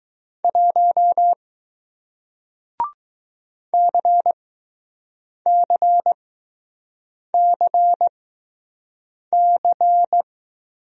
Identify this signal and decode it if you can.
Morse code, heard as a keyed tone, about 21 wpm, spaced as for 6 wpm, 700 Hz. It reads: 1 CCCC